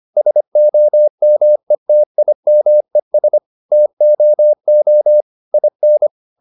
Morse code, heard as a keyed tone, about 25 wpm, 600 Hz. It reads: SOMETIMES TOO IN
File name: the old man and the sea part 1 25wpm